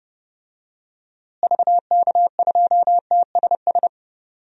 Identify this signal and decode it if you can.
Morse code, heard as a keyed tone, about 30 wpm, 700 Hz. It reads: VK2TSS
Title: DXCC-100 Call Signs - 6 Characters - Repeated 2x - 30wpm